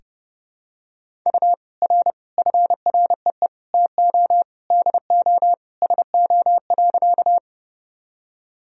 Morse code, heard as a keyed tone, about 30 wpm, 700 Hz. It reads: U R FREE TO DO SO.
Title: Sentences from Top 500 Words and Abbreviations 30wpm